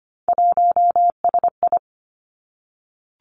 Morse code, heard as a keyed tone, about 25 wpm, 700 Hz. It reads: 1SI
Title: Three Challenging Letter-Number Combinations 25wpm